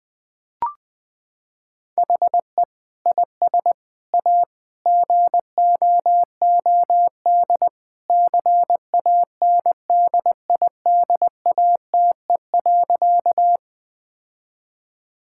Morse code, heard as a keyed tone, about 20 wpm, 700 Hz. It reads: HE IS A GOOD CANDIDATE.
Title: Sentences from Top 1600 Words 20wpm